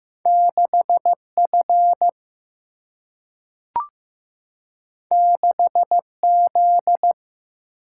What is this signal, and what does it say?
Morse code, heard as a keyed tone, about 15 wpm, 700 Hz. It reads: 6F 6Z
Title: Single-Letter Single-Number 15wpm